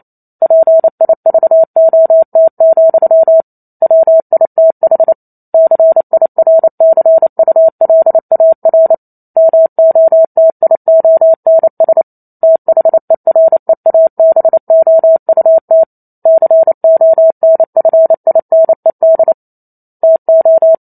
Morse code, heard as a keyed tone, about 28 wpm, 650 Hz. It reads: PIVOT, WITH CIRCULAR MOTIONS THEREABOUT CONFINED TO
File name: Chapter 2 - 28 WPM